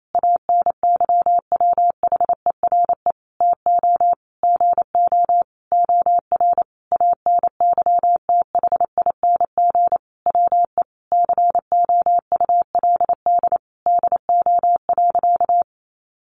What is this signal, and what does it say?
Morse code, heard as a keyed tone, about 28 wpm, 700 Hz. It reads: ANYWHERE TO GO OR ANYTHING WE COULD DO.
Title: Sentences from Top 1400 Words 28wpm